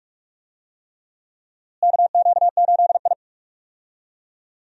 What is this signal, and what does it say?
Morse code, heard as a keyed tone, about 42 wpm, 700 Hz. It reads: KQ8I
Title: USA Call Signs - 4 Characters - Repeated 3x - 45wpm